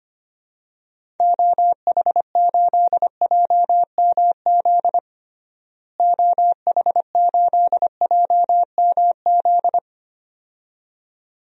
Morse code, heard as a keyed tone, about 25 wpm, 700 Hz. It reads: OH8JMZ OH8JMZ
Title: DXCC-100 Call Signs - 6 Characters - Repeated 2x - 25wpm